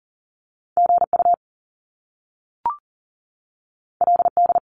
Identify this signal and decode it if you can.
Morse code, heard as a keyed tone, about 43 wpm, 700 Hz. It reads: GU LD
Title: Top Two Letter Combinations in Words 40wpm